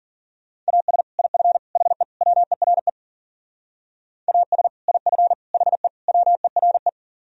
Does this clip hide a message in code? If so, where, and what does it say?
Yes: Morse code from 0.7 to 6.9 s, heard as a keyed tone, about 50 wpm, 700 Hz. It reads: AS IF HE WERE AS IF HE WERE